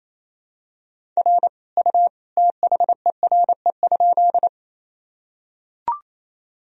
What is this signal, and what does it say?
Morse code, heard as a keyed tone, about 28 wpm, 700 Hz. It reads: R U THERE?
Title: Sentences from Top 100 Words and Abbreviations 28wpm